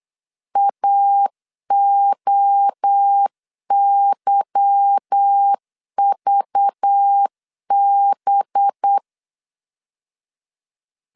Morse code, heard as a keyed tone, about 8 wpm, 800 Hz. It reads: AOYVB
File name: LEZIONI / 9 2 8wpm 3437